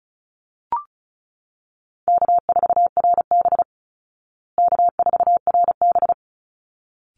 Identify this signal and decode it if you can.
Morse code, heard as a keyed tone, about 34 wpm, 700 Hz. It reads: K4RB K4RB
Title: Easy Call Signs Repeated 2x - 35wpm